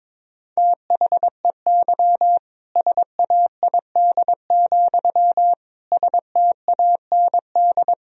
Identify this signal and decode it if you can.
Morse code, heard as a keyed tone, about 22 wpm, 700 Hz. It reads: THEY SAID, STAND
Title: Sentences from Top 1000 Words - Review Part 1 - 22wpm